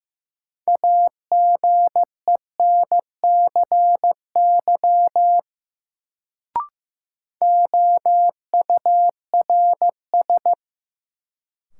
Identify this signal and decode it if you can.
Morse code, heard as a keyed tone, about 15 wpm, 700 Hz. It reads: AGENCY OURS
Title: Sentences from Top 1300 Words - New Word Drill Alt - 15wpm